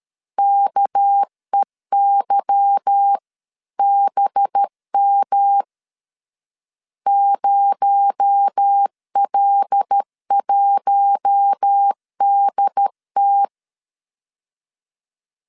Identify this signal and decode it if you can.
Morse code, heard as a keyed tone, about 13 wpm, 800 Hz. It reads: KEYBM 0L1DT
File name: LEZIONI / 8 YJ45 / 8 Y 12wpm 2228